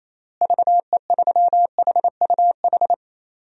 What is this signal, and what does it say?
Morse code, heard as a keyed tone, about 28 wpm, 700 Hz. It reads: VE3HUH
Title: DXCC-100 Call Signs - 6 Characters - Repeated 3x - 28wpm